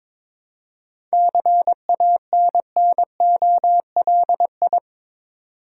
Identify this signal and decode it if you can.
Morse code, heard as a keyed tone, about 22 wpm, 690 Hz. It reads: CANNOLI
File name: Difficult Words with Hints - 22wpm